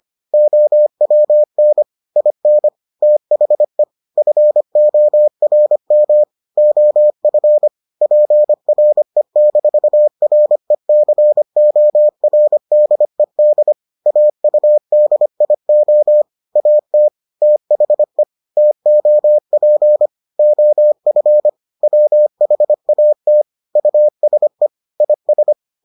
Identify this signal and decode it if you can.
Morse code, heard as a keyed tone, about 25 wpm, 600 Hz. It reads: OWN IN THE FORM OF PRE-RECORDED AUDIO AT THE TOP OF WHAT USE IS